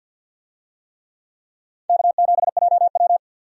Morse code, heard as a keyed tone, about 53 wpm, 700 Hz. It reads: K7JW
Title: USA Call Signs - 4 Characters - 50wpm